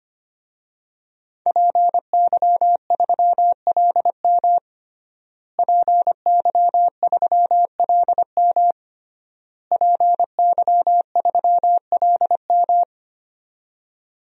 Morse code, heard as a keyed tone, about 25 wpm, 700 Hz. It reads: PY3LM PY3LM PY3LM
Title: DXCC-100 Call Signs - 5 Characters - Repeated 3x - 25wpm